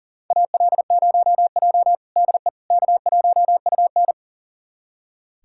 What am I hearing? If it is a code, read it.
Morse code, heard as a keyed tone, about 37 wpm, 700 Hz. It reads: AL0J DE K1UN
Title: USA Callsign DE Callsign - 4 Characters - 40wpm